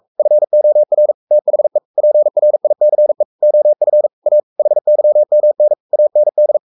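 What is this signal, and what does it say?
Morse code, heard as a keyed tone, about 44 wpm, 600 Hz. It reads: FOR THE PRICE OF A HYMN AND
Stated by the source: Morse code at 43 wpm sending Ken Follett's novel Never in English, from CW med Gnister.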